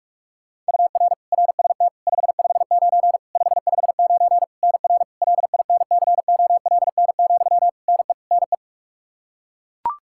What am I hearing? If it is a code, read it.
Morse code, heard as a keyed tone, about 42 wpm, 700 Hz. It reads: UR RST 559 559 NR LINCOLN, NE NE
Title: Practice QSO - Exchange 1 - Beginning - 45wpm